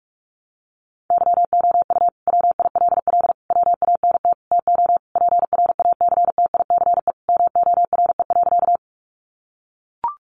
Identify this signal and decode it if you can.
Morse code, heard as a keyed tone, about 44 wpm, 700 Hz. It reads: YOU WILL WANT TO PRACTICE MORE.
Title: Sentences from Top 700 Words 45wpm